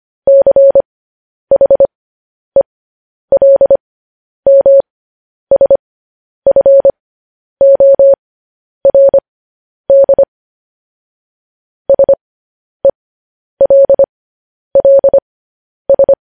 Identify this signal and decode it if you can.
Morse code, heard as a keyed tone, about 25 wpm, spaced as for 10 wpm, 550 Hz. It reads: CHELMSFORD SELLS
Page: news-headlines-in-morse-code-at-10-wpm